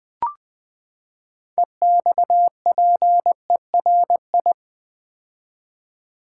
Morse code, heard as a keyed tone, about 20 wpm, 700 Hz. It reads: EXPERI
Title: Build A Word - 6 Letters At A Time - 12-Letter Words - 20wpm